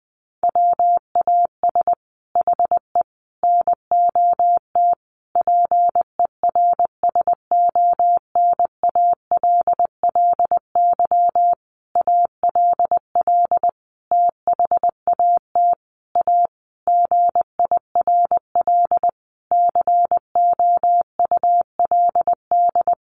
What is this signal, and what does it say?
Morse code, heard as a keyed tone, about 20 wpm, 700 Hz. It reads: WAS HE NOT PERSONALLY ALL THAT A GIRL COULD
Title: Sentences from Top 1400 Words 20wpm